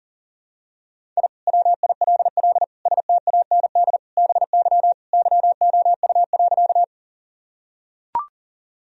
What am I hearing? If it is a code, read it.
Morse code, heard as a keyed tone, about 40 wpm, 700 Hz. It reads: I WILL STAND BY YOU.
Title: Sentences from Top 1000 Words - Review Part 1 - 40wpm